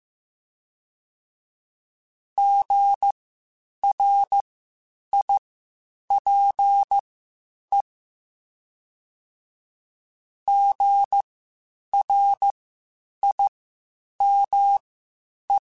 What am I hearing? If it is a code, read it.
Morse code, heard as a keyed tone, about 15 wpm, spaced as for 8 wpm, 800 Hz. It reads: GRIPE GRIME
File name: Spaced Alphas / Spaced Alphas mp3s / Gs sp @13